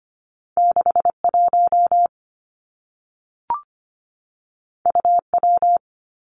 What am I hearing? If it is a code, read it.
Morse code, heard as a keyed tone, about 25 wpm, 700 Hz. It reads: 61 UW
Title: Two Challenging Letter-Number Combinations 25wpm